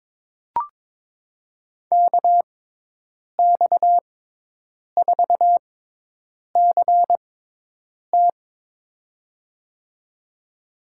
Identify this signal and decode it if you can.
Morse code, heard as a keyed tone, about 21 wpm, spaced as for 8 wpm, 700 Hz. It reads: KX4CT
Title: USA Call Signs - 5 Characters - 6x Character Spacing - 22wpm